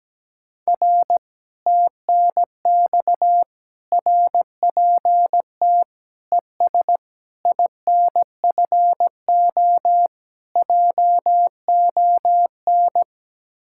Morse code, heard as a keyed tone, about 17 wpm, 700 Hz. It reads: R TNX RPT ES INFO JON
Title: Speed-Racing Common QSO Phrases Encore - 17wpm